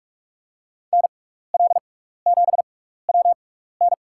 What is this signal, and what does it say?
Morse code, heard as a keyed tone, about 46 wpm, spaced as for 15 wpm, 690 Hz. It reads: NL7WN
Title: USA Call Signs - 5 Characters - 6x Character Spacing - 45wpm